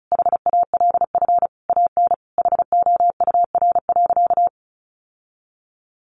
Morse code, heard as a keyed tone, about 34 wpm, 700 Hz. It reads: HALF AN HOUR.
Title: Sentences from Top 1100 Words 35wpm